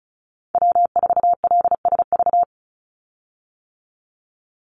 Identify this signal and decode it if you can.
Morse code, heard as a keyed tone, about 34 wpm, 700 Hz. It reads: W4LSV
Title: Call Signs - 35wpm